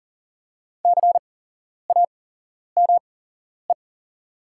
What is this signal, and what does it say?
Morse code, heard as a keyed tone, about 38 wpm, spaced as for 11 wpm, 690 Hz. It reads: CAME
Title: 8x Character Spacing - Top 300 Words - 40wpm